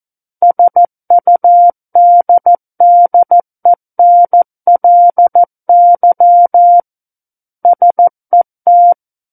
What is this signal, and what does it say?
Morse code, heard as a keyed tone, about 14 wpm, 700 Hz. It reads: SUDDENLY SET
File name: Book One / 05-Chapter 5 - 14 WPM